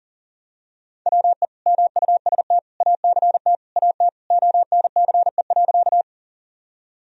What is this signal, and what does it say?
Morse code, heard as a keyed tone, about 40 wpm, 700 Hz. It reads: WE MUST ACT AT ONCE.
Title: Sentences from Top 1000 Words - Review Part 1 - 40wpm